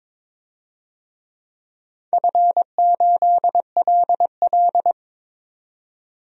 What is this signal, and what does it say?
Morse code, heard as a keyed tone, about 22 wpm, 700 Hz. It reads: F8LL
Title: DXCC-100 Call Signs - 4 Characters - 22wpm